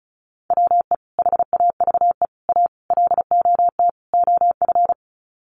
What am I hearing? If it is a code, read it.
Morse code, heard as a keyed tone, about 34 wpm, 700 Hz. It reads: WE HAVE A LOT OF
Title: Sets of 5 Words 35wpm